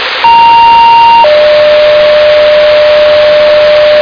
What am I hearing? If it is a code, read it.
Two-tone page, heard as a one-second tone, then a three-second tone, about 930 then 600 Hz, with noise. It is heard from 0.2 to 4.0 s.